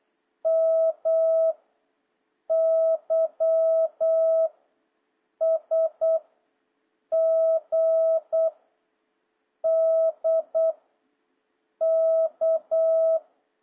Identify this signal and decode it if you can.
Morse code, heard as a keyed tone, about 8 wpm, 650 Hz. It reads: M Y S G D K
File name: jh8cbh / 2026cw